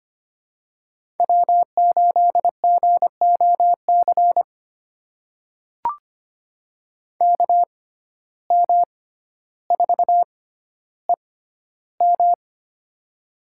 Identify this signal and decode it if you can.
Morse code, heard as a keyed tone, about 25 wpm, spaced as for 13 wpm, 700 Hz. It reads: W8GOC KM4EM
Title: USA Call Signs - 5 Characters - 6x Character Spacing - 25wpm